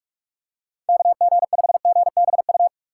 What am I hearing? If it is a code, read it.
Morse code, heard as a keyed tone, about 45 wpm, 700 Hz. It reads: KG5GBU
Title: USA Call Signs - 6 Characters - 3x Character Spacing - 45wpm